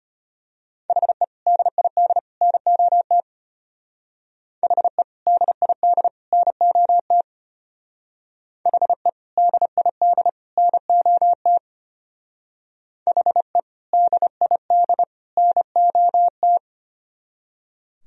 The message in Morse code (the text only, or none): HE DID NOT HE DID NOT HE DID NOT HE DID NOT